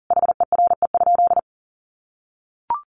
Morse code, heard as a keyed tone, about 40 wpm, 700 Hz. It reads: HERE?
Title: Sentences from Top 1300 Words 40wpm